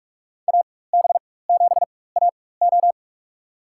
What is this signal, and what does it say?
Morse code, heard as a keyed tone, about 42 wpm, spaced as for 20 wpm, 700 Hz. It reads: AB7AO
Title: USA Call Signs - 5 Characters - 4x Character Spacing - 45wpm